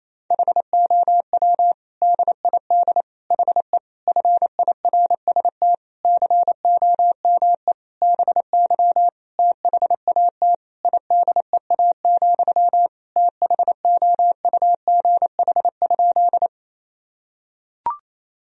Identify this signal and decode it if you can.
Morse code, heard as a keyed tone, about 28 wpm, 700 Hz. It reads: HOW DID HE FIRST COME BY THAT IDEA, THOUGH?
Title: Sentences from Top 1000 Words - Review Part 2 - 28wpm